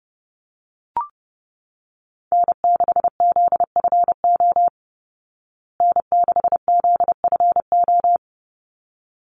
Morse code, heard as a keyed tone, about 30 wpm, 700 Hz. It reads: N6ZFO N6ZFO
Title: Call Signs Repeated 3x - 30wpm